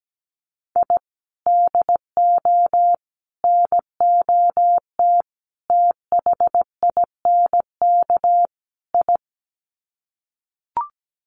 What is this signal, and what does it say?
Morse code, heard as a keyed tone, about 17 wpm, 700 Hz. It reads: I DO NOT THINK I
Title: Sets of 5 Words 17wpm